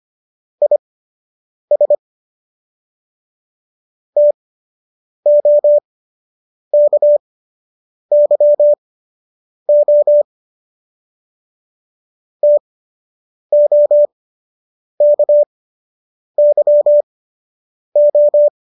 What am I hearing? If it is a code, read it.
Morse code, heard as a keyed tone, about 25 wpm, spaced as for 8 wpm, 590 Hz. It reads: IS TOKYO TOKYO
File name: qso003 08